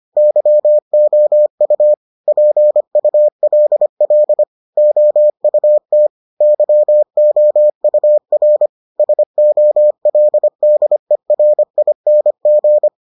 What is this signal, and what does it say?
Morse code, heard as a keyed tone, about 25 wpm, 600 Hz. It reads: YOU PULL OUT YOUR SOLDERING